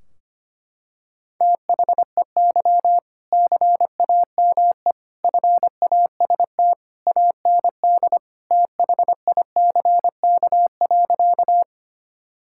Morse code, heard as a keyed tone, about 25 wpm, 700 Hz. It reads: THEY CAME FAST AND THICK.
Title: Sentences from Top 1600 Words 25wpm